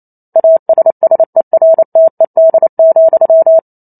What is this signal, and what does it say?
Morse code, heard as a keyed tone, about 28 wpm, 650 Hz. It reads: ASSERTED,